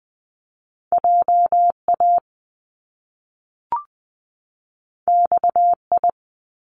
Morse code, heard as a keyed tone, about 20 wpm, 700 Hz. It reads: J A X I
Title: Top Two Letter Combinations in Words 20wpm